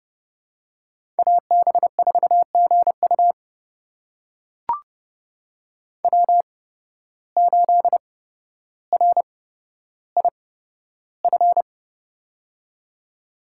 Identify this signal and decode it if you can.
Morse code, heard as a keyed tone, about 30 wpm, spaced as for 13 wpm, 700 Hz. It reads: AB4GU W8RIF
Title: USA Call Signs - 5 Characters - 8x Character Spacing - 30wpm